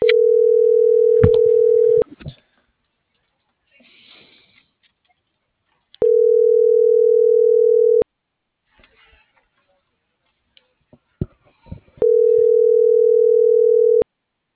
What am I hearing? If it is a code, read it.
Ringback tone (North America), 440 + 480 Hz.